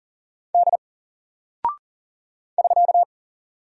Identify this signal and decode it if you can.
Morse code, heard as a keyed tone, about 40 wpm, 700 Hz. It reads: D <SK>
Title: Single Character (Letter:Number:Punctuation:Prosign) 40wpm